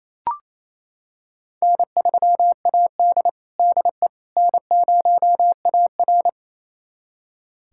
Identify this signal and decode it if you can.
Morse code, heard as a keyed tone, about 27 wpm, 700 Hz. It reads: N3AD DE N0AR
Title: USA Callsign DE Callsign - 4 Characters - 28wpm